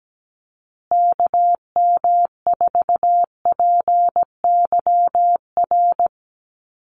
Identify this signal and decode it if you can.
Morse code, heard as a keyed tone, about 17 wpm, 700 Hz. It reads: KM4PYR
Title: USA Call Signs Repeated 2x 17wpm